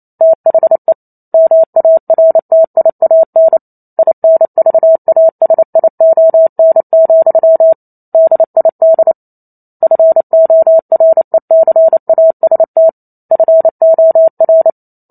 Morse code, heard as a keyed tone, about 28 wpm, 650 Hz. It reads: THE MARTIAN INVASION, DID FORECAST FOR